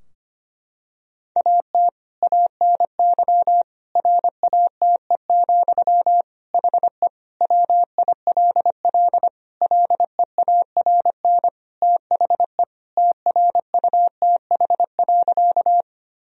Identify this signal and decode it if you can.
Morse code, heard as a keyed tone, about 25 wpm, 700 Hz. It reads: AT ANY RATE, HE WILL LEARN THE TRUTH.
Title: Sentences from Top 1000 Words - Review Part 4 - 25wpm